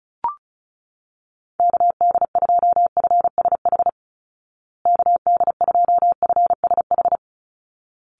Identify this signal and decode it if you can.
Morse code, heard as a keyed tone, about 34 wpm, 700 Hz. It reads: KD2FSH KD2FSH
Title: Call Signs Repeated 3x - 35wpm